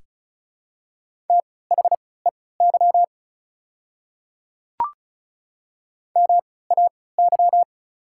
Morse code, heard as a keyed tone, about 36 wpm, spaced as for 20 wpm, 700 Hz. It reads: THEY MAY
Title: 3x Character Spacing - Top 100 Words - 35wpm